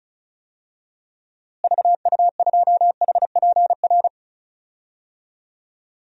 Morse code, heard as a keyed tone, about 36 wpm, 700 Hz. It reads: VU2HPR
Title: DXCC-100 Call Signs - 6 Characters - Repeated 2x - 35wpm